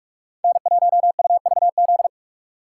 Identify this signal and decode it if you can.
Morse code, heard as a keyed tone, about 46 wpm, 700 Hz. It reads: N1UVZ
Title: USA Call Signs - 5 Characters - 2x Character Spacing - 45wpm